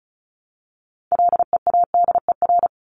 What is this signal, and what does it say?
Morse code, heard as a keyed tone, about 34 wpm, 700 Hz. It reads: LEADER
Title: Top 1501-1750 Words 35wpm